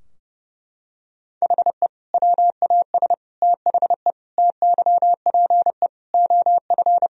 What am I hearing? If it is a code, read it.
Morse code, heard as a keyed tone, about 30 wpm, 700 Hz. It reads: HE WAS THE TYPE OF
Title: Sentences from Top 1000 Words - Review Part 3 - 30wpm